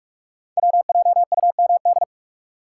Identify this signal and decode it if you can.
Morse code, heard as a keyed tone, about 44 wpm, 690 Hz. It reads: WJUMD
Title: Code Groups - 5 Letters - 45wpm